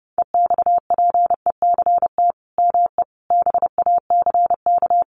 Morse code, heard as a keyed tone, about 30 wpm, 700 Hz. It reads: EXPECT ME BACK
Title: Sentences from Top 1100 Words 30wpm